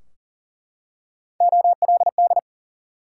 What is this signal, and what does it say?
Morse code, heard as a keyed tone, about 40 wpm, 700 Hz. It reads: OLD
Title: Three Letter Words - 40wpm